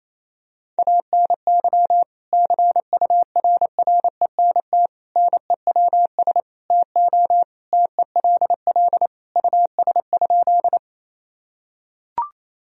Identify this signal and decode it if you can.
Morse code, heard as a keyed tone, about 28 wpm, 700 Hz. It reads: ANY CURRENT NEWS TO TELL US?
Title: Sentences from Top 1000 Words - Review Part 4 - 28wpm